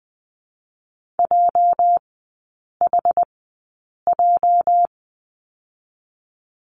Morse code, heard as a keyed tone, about 20 wpm, spaced as for 8 wpm, 700 Hz. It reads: JHJ
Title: 205 Lesson 33e - Letter j - Three Characters 20wpm